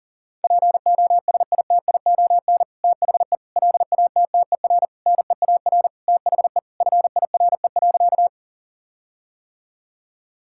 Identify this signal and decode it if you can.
Morse code, heard as a keyed tone, about 40 wpm, 700 Hz. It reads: POSITION THE LATTER NEAR THE FIRE.
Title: Sentences from Top 1600 Words 40wpm